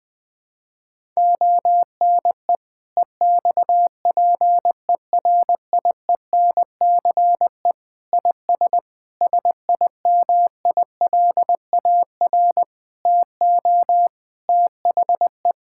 Morse code, heard as a keyed tone, about 21 wpm, 700 Hz. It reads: ONE EXPERIENCE IS SIMILAR TO THE
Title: Sentences from Top 1000 Words - Review Part 5 - 20wpm